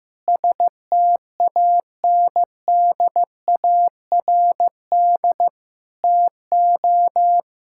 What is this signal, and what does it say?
Morse code, heard as a keyed tone, about 15 wpm, 700 Hz. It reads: STANDARD TO
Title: Sentences from Top 1000 Words - Review Part 6 - 15wpm